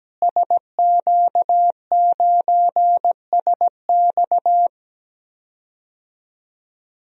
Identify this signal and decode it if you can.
Morse code, heard as a keyed tone, about 17 wpm, 700 Hz. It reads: SQ9SX
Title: DXCC-100 Call Signs - 5 Characters - Repeated 3x - 17wpm